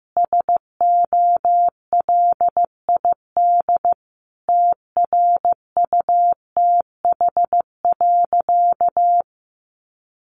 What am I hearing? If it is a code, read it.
Morse code, heard as a keyed tone, about 15 wpm, 700 Hz. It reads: SOLID TRUTH.
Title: Sentences from Top 1500 Words 15wpm